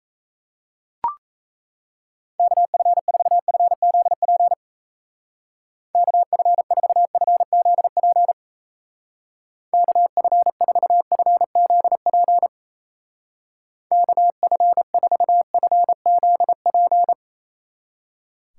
Morse code, heard as a keyed tone, about 35 wpm, 700 Hz. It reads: KF4FZP KF4FZP KF4FZP KF4FZP
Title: Speed Racing - USA 2x3 Call Signs 28wpm